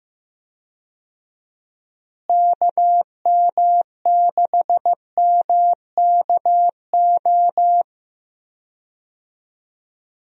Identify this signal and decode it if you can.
Morse code, heard as a keyed tone, about 15 wpm, 700 Hz. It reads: KM6MKO
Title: USA Call Signs - 6 Characters - Repeated 3x - 15wpm